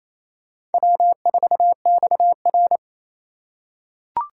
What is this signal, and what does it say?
Morse code, heard as a keyed tone, about 28 wpm, 700 Hz. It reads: W4XR
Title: USA Call Signs - 4 Characters - 3x Character Spacing - 28wpm